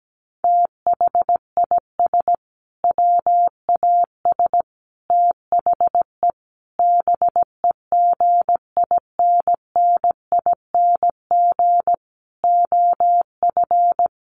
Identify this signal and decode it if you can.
Morse code, heard as a keyed tone, about 17 wpm, 700 Hz. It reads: THIS WAS THE BEGINNING OF
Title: Sentences from Top 900 Words 17wpm